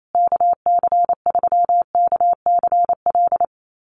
Morse code, heard as a keyed tone, about 28 wpm, 700 Hz. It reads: KC3KCL
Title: USA Call Signs Repeated 3x 28wpm